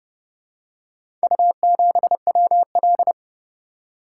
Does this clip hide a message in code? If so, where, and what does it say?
Yes: Morse code from 1.2 to 3.1 s, heard as a keyed tone, about 28 wpm, 700 Hz. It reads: U7WL